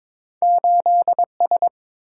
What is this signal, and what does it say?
Morse code, heard as a keyed tone, about 22 wpm, 700 Hz. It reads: 8S
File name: practice / 22wpm / Two Letter-Number Combinations - 22wpm